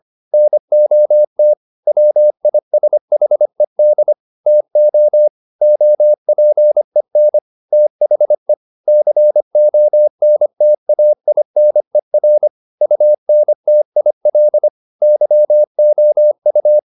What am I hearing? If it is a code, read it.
Morse code, heard as a keyed tone, about 25 wpm, 600 Hz. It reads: NOT WISHED TO OPEN THE CONTAINER UNTIL YOU
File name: the old man and the sea part 1 25wpm